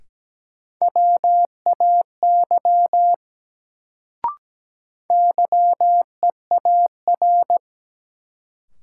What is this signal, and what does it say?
Morse code, heard as a keyed tone, about 17 wpm, 700 Hz. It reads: WAY YEAR
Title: Sets of 2 Words - New Word Drill - 17wpm